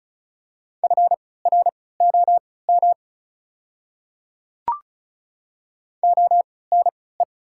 Morse code, heard as a keyed tone, about 36 wpm, spaced as for 20 wpm, 690 Hz. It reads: FROM ONE
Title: 3x Character Spacing - Top 100 Words - 35wpm